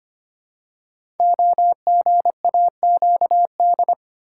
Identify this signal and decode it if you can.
Morse code, heard as a keyed tone, about 25 wpm, 700 Hz. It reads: OGAQD
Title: Code Groups - 5 Letters - 25wpm